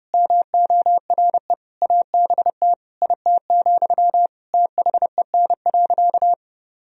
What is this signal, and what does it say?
Morse code, heard as a keyed tone, about 30 wpm, 700 Hz. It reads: MORE ABT IT, THEN.
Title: Sentences from Top 200 Words and Abbreviations 30wpm